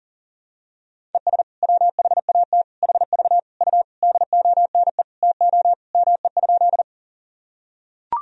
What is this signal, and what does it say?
Morse code, heard as a keyed tone, about 39 wpm, 700 Hz. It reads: ES WHAT HV U DONE TO ME?
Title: Sentences from Top 500 Words and Abbreviations 40wpm